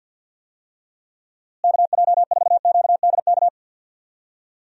Morse code, heard as a keyed tone, about 48 wpm, 700 Hz. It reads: KJ4QDK
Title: USA Call Signs - 6 Characters - Repeated 3x - 50wpm